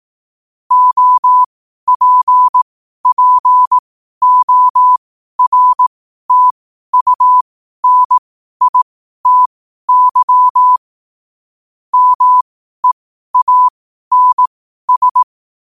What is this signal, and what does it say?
Morse code, heard as a keyed tone, about 18 wpm, spaced as for 12 wpm, 1000 Hz. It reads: OPPORTUNITY MEANS
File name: qotdfiles / 12wpm